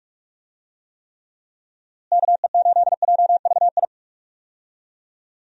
Morse code, heard as a keyed tone, about 42 wpm, 700 Hz. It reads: KE8JVI
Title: USA Call Signs - 6 Characters - Repeated 2x - 45wpm